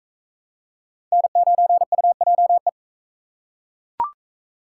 Morse code, heard as a keyed tone, about 39 wpm, 700 Hz. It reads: N9UJE